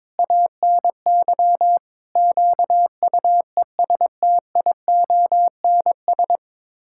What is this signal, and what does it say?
Morse code, heard as a keyed tone, about 21 wpm, 700 Hz. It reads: ANY QUESTIONS
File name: practice / 22wpm / Sets of 8 Words - 22wpm